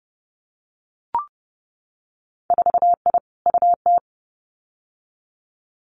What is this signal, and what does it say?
Morse code, heard as a keyed tone, about 30 wpm, 700 Hz. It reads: 4I UT
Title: Winter Field Day Exchanges 30wpm